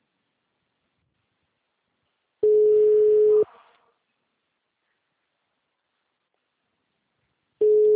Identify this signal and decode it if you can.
Ringback tone (Europe, 425 Hz).